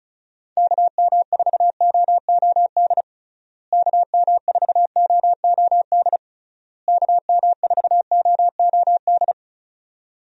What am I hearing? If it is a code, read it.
Morse code, heard as a keyed tone, about 36 wpm, 700 Hz. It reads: KM4OOD KM4OOD KM4OOD
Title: USA Call Signs - 6 Characters - Repeated 3x - 35wpm